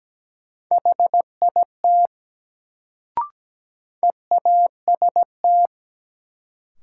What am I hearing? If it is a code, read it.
Morse code, heard as a keyed tone, about 17 wpm, 700 Hz. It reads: HIT EAST
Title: Sentences from Top 1200 Words - New Word Drill Alt - 17wpm